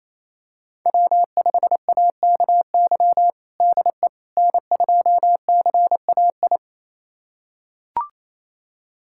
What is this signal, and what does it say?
Morse code, heard as a keyed tone, about 28 wpm, 700 Hz. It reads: W5AKY DE N2CAI
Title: USA Callsign DE Callsign - 5 Characters - 28wpm